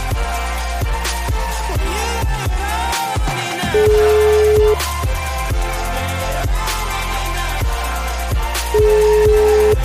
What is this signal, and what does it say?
Ringback tone (Europe, 425 Hz), with noise.